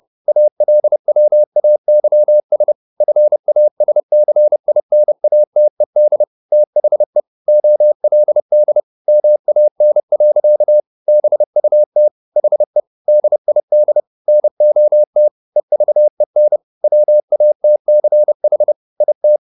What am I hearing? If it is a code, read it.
Morse code, heard as a keyed tone, about 30 wpm, 600 Hz. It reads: ALWAYS FASCINATED THE OLD MAN. BUT HE DID NOT EVEN WATCH IT